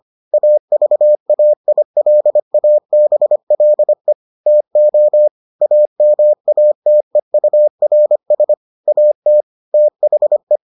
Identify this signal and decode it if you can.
Morse code, heard as a keyed tone, about 25 wpm, 600 Hz. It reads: AVAILABLE TO AMATEURS AT THE